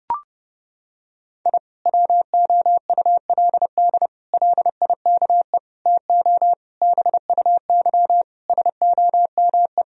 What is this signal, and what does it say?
Morse code, heard as a keyed tone, about 32 wpm, 700 Hz. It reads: I WOULD LIKE TO BUY SOME
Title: Sentences from Top 1000 Words - Review Part 4 - 30wpm